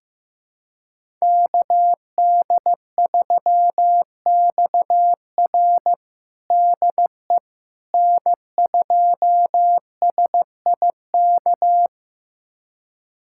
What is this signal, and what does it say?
Morse code, heard as a keyed tone, about 15 wpm, 700 Hz. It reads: KD3XR DE N2SIK
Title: USA Callsign DE Callsign - 5 Characters - 15wpm